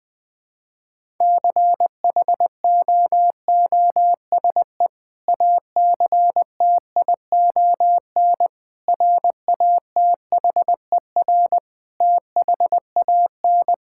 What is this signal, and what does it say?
Morse code, heard as a keyed tone, about 21 wpm, 700 Hz. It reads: CHOOSE ACTION RATHER THAN
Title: Sentences with 2-Word Phrases - Spoken Beginning - 20wpm